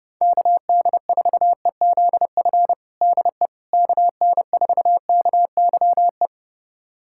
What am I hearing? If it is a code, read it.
Morse code, heard as a keyed tone, about 30 wpm, 690 Hz. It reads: KD4EZF DE KN4KYE
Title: USA Callsign DE Callsign 30wpm